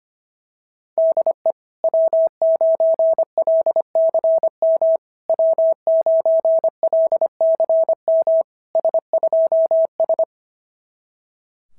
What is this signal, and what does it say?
Morse code, heard as a keyed tone, about 25 wpm, 690 Hz. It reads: DE W9LCM W9LCM S2S
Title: SOTA QSO Practice 25wpm